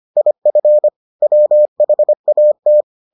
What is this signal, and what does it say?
Morse code, heard as a keyed tone, about 25 wpm, 600 Hz. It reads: IF WHAT